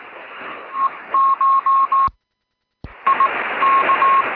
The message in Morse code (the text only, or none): E CH I K